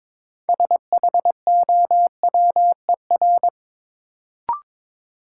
Morse code, heard as a keyed tone, about 22 wpm, 700 Hz. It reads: SHOWER
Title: Sentences from Top 1600 Words - New Word Drill - 22wpm